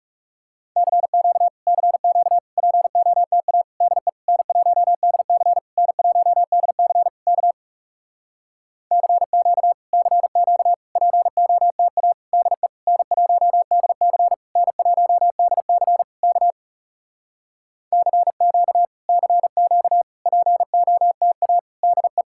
Morse code, heard as a keyed tone, about 40 wpm, 700 Hz. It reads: CQ CQ POTA DE N1DC N1DC K CQ CQ POTA DE N1DC N1DC K CQ CQ POTA DE